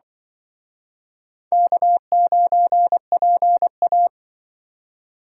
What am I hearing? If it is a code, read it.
Morse code, heard as a keyed tone, about 24 wpm, 700 Hz. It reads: K9PA
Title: Speed Racing - USA Call Signs 20wpm